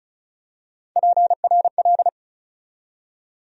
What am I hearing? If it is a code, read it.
Morse code, heard as a keyed tone, about 34 wpm, 700 Hz. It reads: PRL
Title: USA Call Sign Postfixes - 3 Letters 35wpm